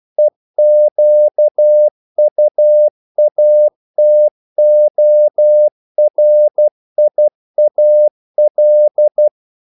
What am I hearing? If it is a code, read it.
Morse code, heard as a keyed tone, about 12 wpm, 600 Hz. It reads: EQUATORIAL
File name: CW / Practice / NationalCapitals12